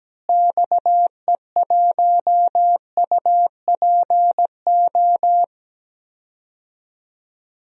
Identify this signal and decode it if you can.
Morse code, heard as a keyed tone, about 17 wpm, 700 Hz. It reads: XE1UPO
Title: DXCC-100 Call Signs - 6 Characters - Repeated 3x - 17wpm